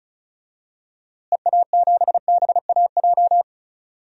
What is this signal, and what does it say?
Morse code, heard as a keyed tone, about 36 wpm, 700 Hz. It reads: EA7BAJ
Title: DXCC-100 Call Signs - 6 Characters - Repeated 3x - 35wpm